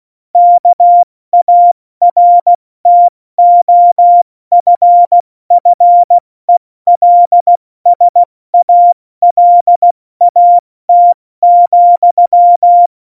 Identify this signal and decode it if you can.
Morse code, heard as a keyed tone, about 16 wpm, 700 Hz. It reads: KARTOFFELSALAT,
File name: Morsekurs / Witze / Witz 03. November 2020, 70 F80